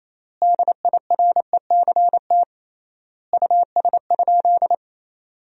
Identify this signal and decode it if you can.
Morse code, heard as a keyed tone, about 28 wpm, 700 Hz. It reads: DIRECT US?
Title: 3x Word-Spacing - Sentences from Top 400 Words - 28wpm